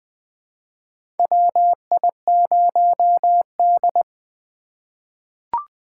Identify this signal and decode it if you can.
Morse code, heard as a keyed tone, about 20 wpm, 700 Hz. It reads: WI0D
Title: USA Call Signs - 4 Characters - 4x Character Spacing - 20wpm